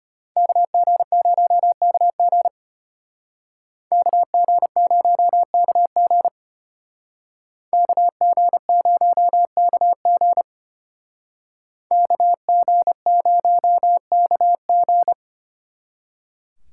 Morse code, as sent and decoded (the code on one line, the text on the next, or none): -.- --. ----- -.- --. / -.- --. ----- -.- --. / -.- --. ----- -.- --. / -.- --. ----- -.- --.
KG0KG KG0KG KG0KG KG0KG